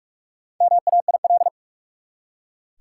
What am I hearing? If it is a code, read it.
Morse code, heard as a keyed tone, about 46 wpm, 700 Hz. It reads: MAIL
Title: Four Letter Words - 45wpm